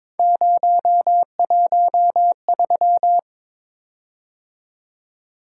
Morse code, heard as a keyed tone, about 22 wpm, 700 Hz. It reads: 013
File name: practice / 22wpm / Three Letter-Number Combinations - 22wpm